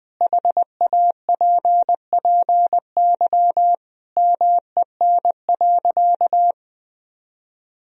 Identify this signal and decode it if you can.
Morse code, heard as a keyed tone, about 20 wpm, 700 Hz. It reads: HAPPY MEN.
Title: Sentences from Top 1000 Words - Review Part 1 - 20wpm